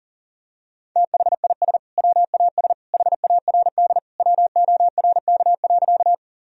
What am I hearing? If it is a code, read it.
Morse code, heard as a keyed tone, about 37 wpm, 700 Hz. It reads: THIS WAS HARD WORK.
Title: Sentences from Top 1000 Words - Review Part 1 - 40wpm